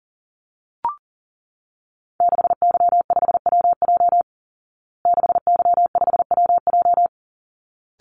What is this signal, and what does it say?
Morse code, heard as a keyed tone, about 40 wpm, 700 Hz. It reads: 6Y5WJ 6Y5WJ
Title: Call Signs Repeated 3x - 40wpm